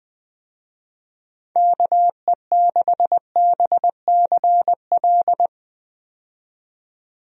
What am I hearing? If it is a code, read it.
Morse code, heard as a keyed tone, about 20 wpm, 700 Hz. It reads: KE6BCL
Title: USA Call Signs - 6 Characters - 20wpm